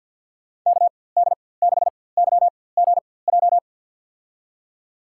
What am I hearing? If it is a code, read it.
Morse code, heard as a keyed tone, about 51 wpm, spaced as for 23 wpm, 700 Hz. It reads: KD6YGJ
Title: USA Call Signs - 6 Characters - 4x Character Spacing - 50wpm